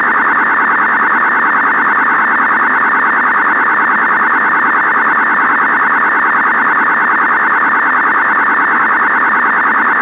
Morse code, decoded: none

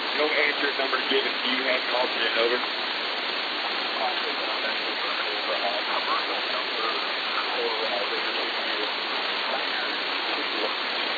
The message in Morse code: none

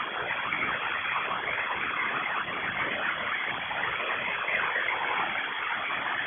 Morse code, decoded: none

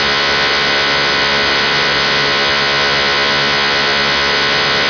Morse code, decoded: none

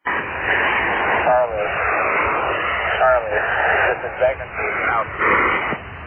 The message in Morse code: none

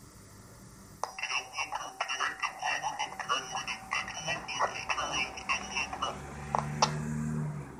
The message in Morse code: none

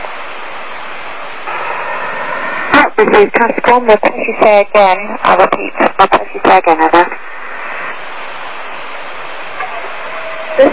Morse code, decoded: none

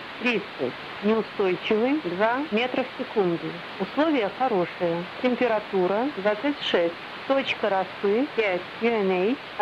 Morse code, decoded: none

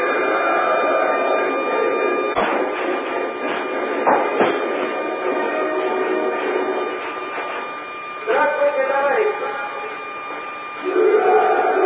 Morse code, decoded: none

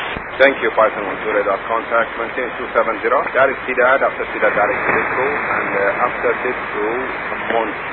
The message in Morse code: none

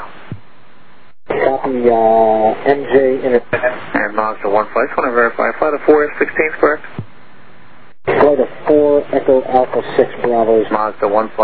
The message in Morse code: none

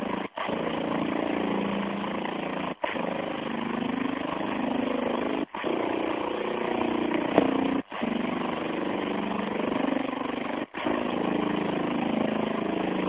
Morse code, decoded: none

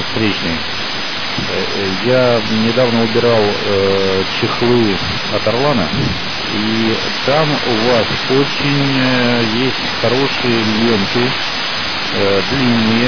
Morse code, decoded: none

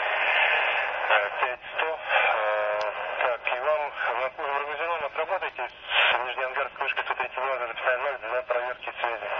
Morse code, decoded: none

